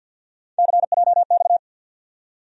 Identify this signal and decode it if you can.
Morse code, heard as a keyed tone, about 53 wpm, 700 Hz. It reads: CJX
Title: USA Call Sign Postfixes - 3 Letters 50wpm